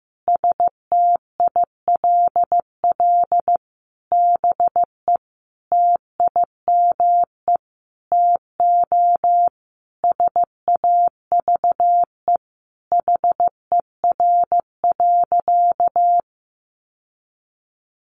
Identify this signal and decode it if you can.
Morse code, heard as a keyed tone, about 15 wpm, 700 Hz. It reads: STILL BE TIME TO SAVE HER.
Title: Sentences from Top 1100 Words 15wpm